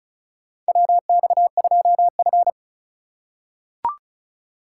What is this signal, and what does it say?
Morse code, heard as a keyed tone, about 36 wpm, 700 Hz. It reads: WX2F
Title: USA Call Signs - 2x Character Spacing - 35wpm